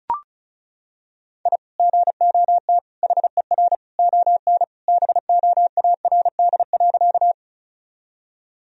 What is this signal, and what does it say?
Morse code, heard as a keyed tone, about 36 wpm, 700 Hz. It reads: I GOT HER ON BOARD.
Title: Sentences from Top 1000 Words - Review Part 4 - 35wpm